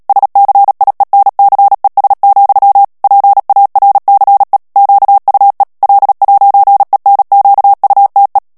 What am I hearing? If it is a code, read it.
Morse code, heard as a keyed tone, about 37 wpm, 800 Hz. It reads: SCIENCES, PARCE QUE L'ENQUTE
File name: CW / FICHIERS-MP3 / FRANCAIS TEXTE CLAIR 1 VITESSE 35 MOTS MINUTES